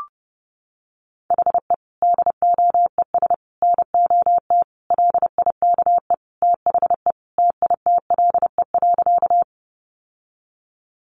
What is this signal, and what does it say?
Morse code, heard as a keyed tone, about 30 wpm, 700 Hz. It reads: HE DOES NOT LIKE THE TITLE.
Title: Sentences from Top 1200 Words 30wpm